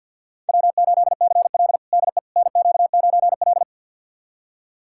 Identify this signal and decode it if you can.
Morse code, heard as a keyed tone, about 51 wpm, 700 Hz. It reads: W8CL DE NQ9L
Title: USA Callsign DE Callsign - 4 Characters - 50wpm